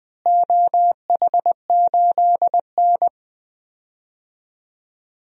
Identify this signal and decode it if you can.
Morse code, heard as a keyed tone, about 20 wpm, 700 Hz. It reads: OH8N
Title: DXCC-100 Call Signs - 4 Characters - Repeated 3x - 20wpm